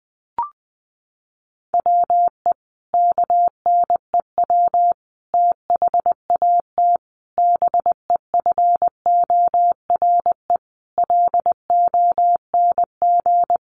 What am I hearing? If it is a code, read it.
Morse code, heard as a keyed tone, about 20 wpm, 700 Hz. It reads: WE KNEW THAT BEFORE LONG
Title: Sentences from Top 400 Words 20wpm